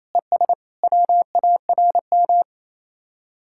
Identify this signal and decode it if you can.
Morse code, heard as a keyed tone, about 27 wpm, 700 Hz. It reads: ES WARM
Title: Speed-Racing Common QSO Phrases - 28wpm